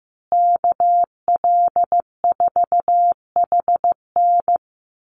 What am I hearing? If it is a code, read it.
Morse code, heard as a keyed tone, about 15 wpm, 700 Hz. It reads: KL4HN
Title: USA Call Signs Repeated 3x 15wpm